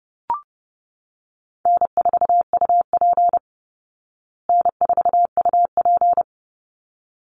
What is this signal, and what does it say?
Morse code, heard as a keyed tone, about 30 wpm, 700 Hz. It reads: N4UP N4UP
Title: Easy Call Signs Repeated 2x - 30wpm